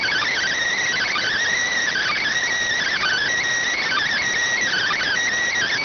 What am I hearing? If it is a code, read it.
Emergency Alert System (SAME) tones (header bursts), heard as harsh buzzing bursts. It is heard from 0.8 to 5.2 s.